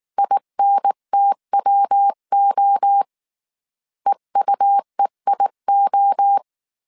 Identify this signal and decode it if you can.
Morse code, heard as a keyed tone, about 19 wpm, 800 Hz. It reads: INTWO EUEIO